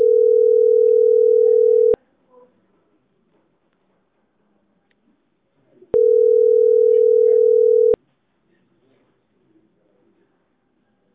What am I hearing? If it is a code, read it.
Ringback tone (North America), 440 + 480 Hz.